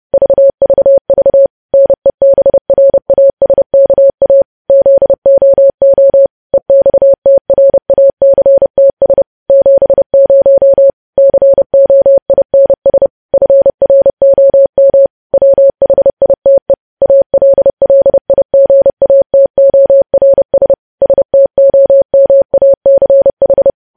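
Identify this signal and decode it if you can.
Morse code, heard as a keyed tone, about 30 wpm, 550 Hz. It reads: VVV NEBRASKA ZOO EXTRACTS 70 COINS FROM WHITE ALLIGATORS STOMACH
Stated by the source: Morse code news headlines sent at 30 WPM.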